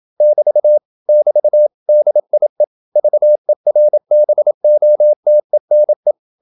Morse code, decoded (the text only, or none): <BT> <BT> DIE VERBOTENE